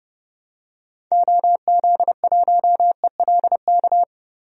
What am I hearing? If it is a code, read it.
Morse code, heard as a keyed tone, about 30 wpm, 700 Hz. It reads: OZ1ELK
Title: DXCC-100 Call Signs - 6 Characters - Repeated 2x - 30wpm